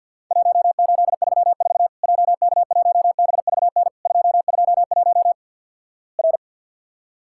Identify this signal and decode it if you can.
Morse code, heard as a keyed tone, about 50 wpm, 700 Hz. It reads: 1834 JK1BVN 221 R
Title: Worked All Europe DX Contest - QTC Exchange with Repeats - 50wpm